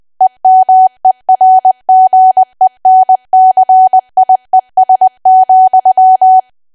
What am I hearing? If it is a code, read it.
Morse code, heard as a keyed tone, about 20 wpm, 750 Hz. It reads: EMERGENCIES,
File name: cwa / W1AW-Level3 / 20WPM